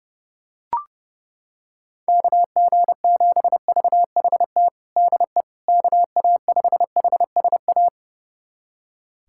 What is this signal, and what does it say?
Morse code, heard as a keyed tone, about 30 wpm, 700 Hz. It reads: KG7VHT DE KA5HSA
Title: USA Callsign DE Callsign - 6 Characters - 30wpm